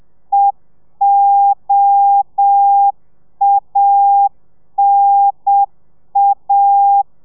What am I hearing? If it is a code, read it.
Morse code, heard as a keyed tone, about 7 wpm, 800 Hz. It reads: EOANA